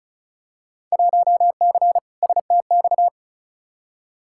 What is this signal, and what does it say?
Morse code, heard as a keyed tone, about 36 wpm, 700 Hz. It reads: 1C STX
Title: Field Day Exchanges Encore - 35wpm